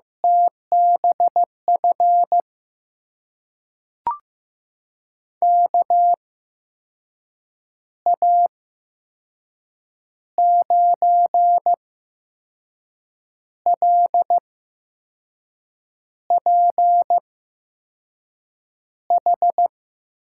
Morse code, heard as a keyed tone, about 15 wpm, spaced as for 5 wpm, 700 Hz. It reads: TBF KA9LPH
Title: USA Call Signs - 6 Characters - 8x Character Spacing - 15wpm